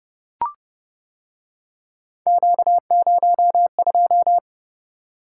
Morse code, heard as a keyed tone, about 29 wpm, 700 Hz. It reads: Q02